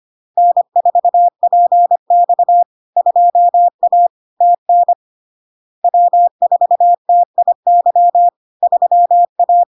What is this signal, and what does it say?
Morse code, heard as a keyed tone, about 25 wpm, 700 Hz. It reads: N4PX 2A TN W4TIY 3A
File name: LARK-FD-2022-Log-in-CW-25wpm-700-Hz